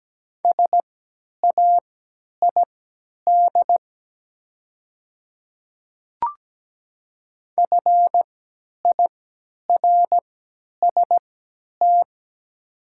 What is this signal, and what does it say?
Morse code, heard as a keyed tone, about 17 wpm, spaced as for 10 wpm, 700 Hz. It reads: SAID FIRST
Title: 3x Character Spacing - Top 100 Words - 17wpm